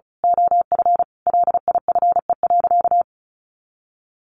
Morse code, heard as a keyed tone, about 34 wpm, 700 Hz. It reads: OF LIFE.
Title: A Sound Of Thunder 35wpm